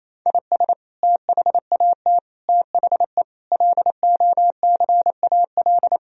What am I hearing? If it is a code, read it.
Morse code, heard as a keyed tone, about 28 wpm, 700 Hz. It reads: IS THAT THE LOCAL
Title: Sentences from Top 1000 Words - Review Part 3 - 28wpm